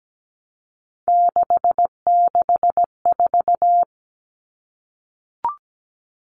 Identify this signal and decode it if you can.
Morse code, heard as a keyed tone, about 17 wpm, 700 Hz. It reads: 664
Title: Common Cut Numbers - 3-Digit 17wpm